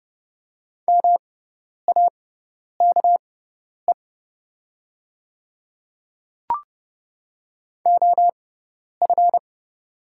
Morse code, heard as a keyed tone, about 29 wpm, spaced as for 10 wpm, 690 Hz. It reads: MAKE OF
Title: 6x Character Spacing - Top 200 Words - 30wpm